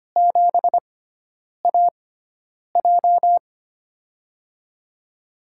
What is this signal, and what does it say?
Morse code, heard as a keyed tone, about 25 wpm, spaced as for 9 wpm, 700 Hz. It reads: 7AJ
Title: USA Call Signs - 4 Characters - 6x Character Spacing - 25wpm